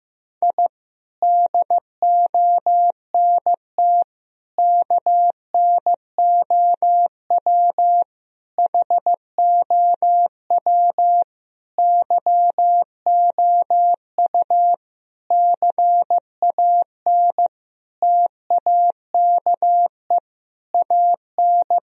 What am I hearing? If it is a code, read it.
Morse code, heard as a keyed tone, about 15 wpm, 700 Hz. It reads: I DONT KNOW HOW YOU CAN TAKE AN
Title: Sentences from Top 1000 Words - Review Part 2 - 15wpm